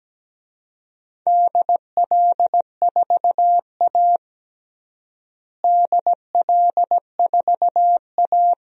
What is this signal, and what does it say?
Morse code, heard as a keyed tone, about 17 wpm, 700 Hz. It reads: DL4A DL4A
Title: DXCC-100 Call Signs - 4 Characters - Repeated 2x - 17wpm